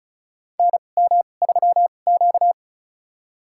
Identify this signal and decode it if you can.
Morse code, heard as a keyed tone, about 36 wpm, spaced as for 25 wpm, 700 Hz. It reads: NM3Q
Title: USA Call Signs - 2x Character Spacing - 35wpm